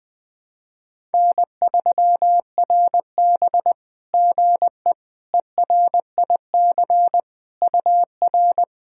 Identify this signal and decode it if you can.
Morse code, heard as a keyed tone, about 21 wpm, 660 Hz. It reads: N3RB GE ERIC UR
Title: SOTA QSO Practice 20wpm